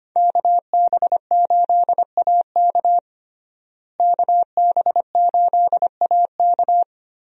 Morse code, heard as a keyed tone, about 25 wpm, 700 Hz. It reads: KB8AK KB8AK
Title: USA Call Signs - 5 Characters - Repeated 3x - 25wpm